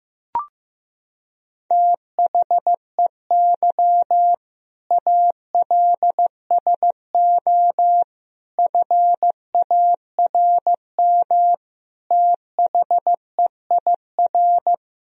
Morse code, heard as a keyed tone, about 15 wpm, 700 Hz. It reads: THEY ALSO FARM THEIR
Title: Sentences from Top 1000 Words - Review Part 2 - 15wpm